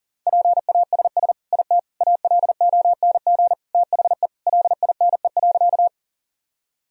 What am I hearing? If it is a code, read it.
Morse code, heard as a keyed tone, about 37 wpm, 700 Hz. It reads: PASS IT ALONG THE LINE.
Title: Sentences from Top 1000 Words - Review Part 1 - 40wpm